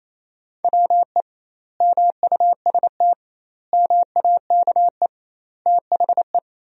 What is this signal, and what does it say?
Morse code, heard as a keyed tone, about 28 wpm, 700 Hz. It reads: WE MUST MAKE THE
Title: 2x Word-Spacing - Sentences from Top 400 Words - 28wpm